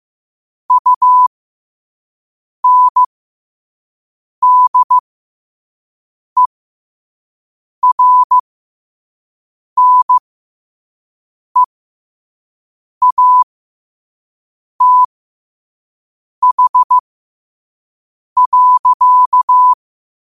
Morse code, heard as a keyed tone, about 15 wpm, spaced as for 5 wpm, 1000 Hz. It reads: UNDERNEATH.
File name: qotdfiles / 5wpm